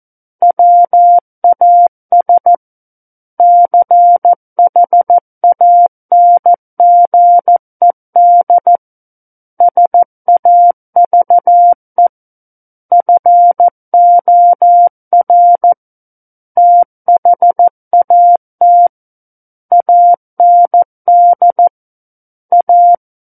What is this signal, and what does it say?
Morse code, heard as a keyed tone, about 14 wpm, 700 Hz. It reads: WAS CHANGED SAVE FOR THAT AND A
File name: Book One / 05-Chapter 5 - 14 WPM